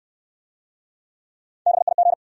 Morse code, heard as a keyed tone, about 68 wpm, 700 Hz. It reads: BIG